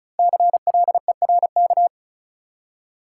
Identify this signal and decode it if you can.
Morse code, heard as a keyed tone, about 34 wpm, 700 Hz. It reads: CLERK
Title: Five Letter Words - 35wpm